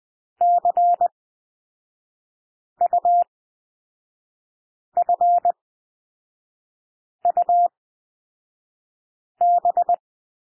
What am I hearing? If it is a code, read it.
Morse code, heard as a keyed tone, about 20 wpm, spaced as for 5 wpm, 700 Hz. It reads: CUFUB
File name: cwmp3 / L2 SOUND LETTERS 20 wpm